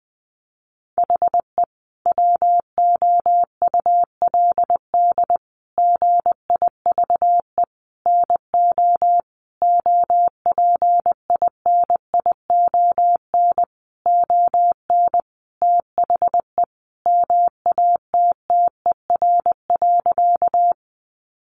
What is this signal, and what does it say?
Morse code, heard as a keyed tone, about 19 wpm, 700 Hz. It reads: HE WOULD GIVE NO OPINION ON THE MATTER.
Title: Sentences from Top 1200 Words 20wpm